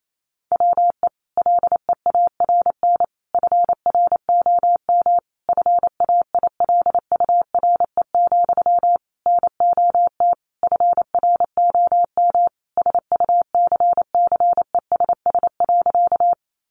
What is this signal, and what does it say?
Morse code, heard as a keyed tone, about 28 wpm, 700 Hz. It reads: WE LEARN FROM FAILURE, NOT FROM SUCCESS.
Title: Sentences from Top 900 Words 28wpm